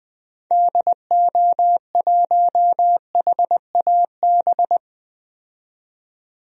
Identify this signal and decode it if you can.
Morse code, heard as a keyed tone, about 20 wpm, 700 Hz. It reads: DO1HAB
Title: DXCC-100 Call Signs - 6 Characters - 20wpm